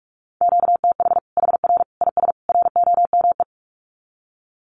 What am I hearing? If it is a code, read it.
Morse code, heard as a keyed tone, about 44 wpm, 700 Hz. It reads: QTH HR IS ROME
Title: Common QSO Phrases 45wpm